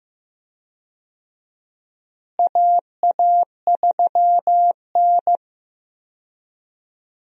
Morse code, heard as a keyed tone, about 15 wpm, 700 Hz. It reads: AA3N
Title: USA Call Signs - 4 Characters - Repeated 2x - 15wpm